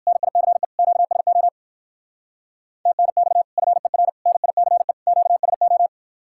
Morse code, heard as a keyed tone, about 54 wpm, 690 Hz. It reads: NICE QSO TNX FER NICE QSO
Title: Speed-Racing Common QSO Phrases - 45wpm